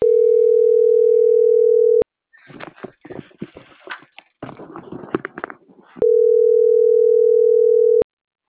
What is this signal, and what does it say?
Ringback tone (North America), 440 + 480 Hz.